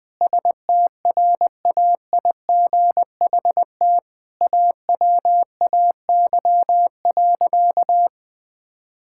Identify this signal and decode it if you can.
Morse code, heard as a keyed tone, about 20 wpm, 700 Hz. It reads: STRAIGHT AWAY.
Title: Sentences from Top 1000 Words - Review Part 6 - 20wpm